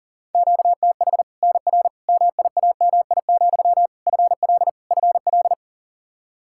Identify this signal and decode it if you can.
Morse code, heard as a keyed tone, about 40 wpm, 700 Hz. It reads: QTH NR MIAMI, FL FL
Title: Practice QSO - Exchange 1 - Beginning - 40wpm